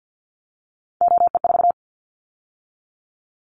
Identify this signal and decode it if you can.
Morse code, heard as a keyed tone, about 50 wpm, 700 Hz. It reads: OE4
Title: Callsign Prefixes for Top 25 DXCC - 50wpm